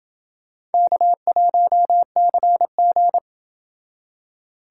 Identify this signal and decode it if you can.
Morse code, heard as a keyed tone, about 28 wpm, 700 Hz. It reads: K1CG